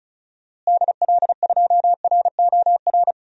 Morse code, heard as a keyed tone, about 36 wpm, 700 Hz. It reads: DL2ROR
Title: DXCC-100 Call Signs - 6 Characters - Repeated 2x - 35wpm